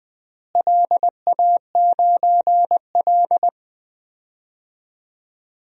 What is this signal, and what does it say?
Morse code, heard as a keyed tone, about 20 wpm, 700 Hz. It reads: LA9L
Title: DXCC-100 Call Signs - 4 Characters - Repeated 2x - 20wpm